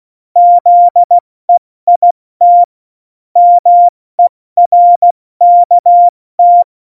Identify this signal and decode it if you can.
Morse code, heard as a keyed tone, about 16 wpm, 700 Hz. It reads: ZEIT MERKT
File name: Morsekurs / Witze / Witz 26 Mai, 2020 70 F80